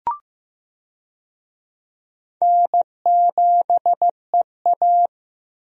Morse code, heard as a keyed tone, about 15 wpm, 700 Hz. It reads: N7EA
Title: USA Call Signs - 4 Characters - 15wpm